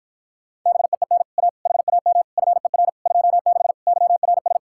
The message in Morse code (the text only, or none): BEEN A HAM FER 26 YRS